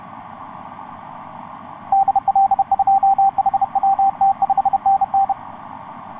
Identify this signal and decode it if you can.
Morse code, heard as a keyed tone, about 30 wpm, 780 Hz, with radio noise. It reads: DL2HWT 5NN
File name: xv9dx ik1avw etc 12m cw 16feb2010 0909